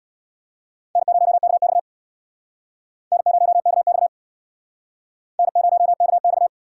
Morse code, heard as a keyed tone, about 64 wpm, 700 Hz. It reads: N0KX N0KX N0KX